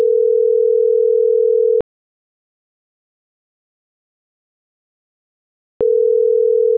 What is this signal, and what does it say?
Ringback tone (North America), 440 + 480 Hz.